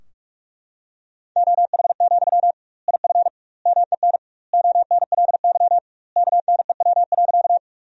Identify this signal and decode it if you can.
Morse code, heard as a keyed tone, about 45 wpm, 700 Hz. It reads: OH, IF MEN ONLY KNEW.
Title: 2x Word-Spacing - Sentences from Top 500 Words - 45wpm